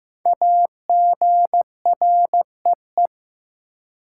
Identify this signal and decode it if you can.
Morse code, heard as a keyed tone, about 15 wpm, 700 Hz. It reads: AGREE
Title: Five Letter Words - 15wpm